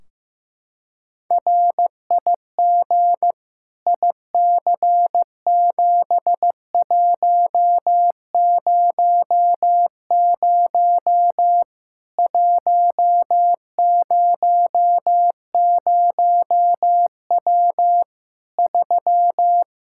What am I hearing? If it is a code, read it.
Morse code, heard as a keyed tone, about 15 wpm, 700 Hz. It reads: RIG IC7100 100W 3
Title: Practice QSO - Exchange 2 - Info - 15wpm